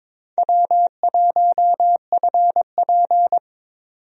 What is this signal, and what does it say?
Morse code, heard as a keyed tone, about 21 wpm, 700 Hz. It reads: W1FP
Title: USA Call Signs - 4 Characters - 22wpm